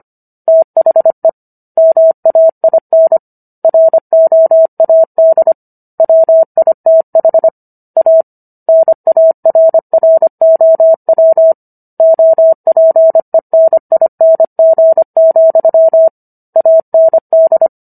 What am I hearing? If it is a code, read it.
Morse code, heard as a keyed tone, about 25 wpm, 650 Hz. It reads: THE MAIN ROAD WITH A NARROW OPENING, AND